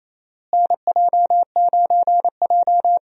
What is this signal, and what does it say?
Morse code, heard as a keyed tone, about 27 wpm, 700 Hz. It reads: NJ9J
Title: USA Call Signs - 4 Characters - Repeated 2x - 28wpm